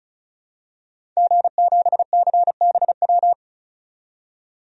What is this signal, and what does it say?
Morse code, heard as a keyed tone, about 36 wpm, 700 Hz. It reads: G7CBW
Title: DXCC-100 Call Signs - 5 Characters - 35wpm